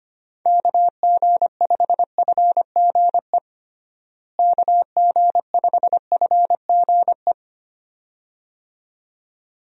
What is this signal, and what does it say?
Morse code, heard as a keyed tone, about 25 wpm, 700 Hz. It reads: KG5FGE KG5FGE
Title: USA Call Signs - 6 Characters - Repeated 2x - 25wpm